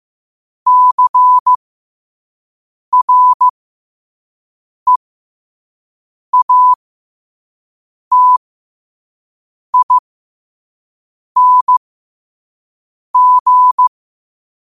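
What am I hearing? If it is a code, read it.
Morse code, heard as a keyed tone, about 15 wpm, spaced as for 5 wpm, 1000 Hz. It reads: CREATING
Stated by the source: Morse code quote of the day at 5 wpm.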